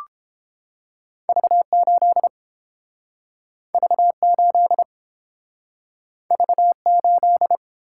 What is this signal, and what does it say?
Morse code, heard as a keyed tone, about 29 wpm, 700 Hz. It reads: V8 V8 V8